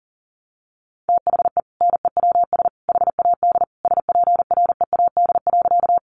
Morse code, heard as a keyed tone, about 40 wpm, 700 Hz. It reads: THE NEWS HAD SPREAD.
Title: Sentences from Top 1200 Words 40wpm